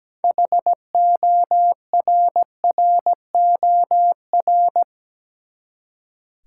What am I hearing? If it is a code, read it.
Morse code, heard as a keyed tone, about 17 wpm, 700 Hz. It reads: HORROR
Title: Six Letter Words - 17wpm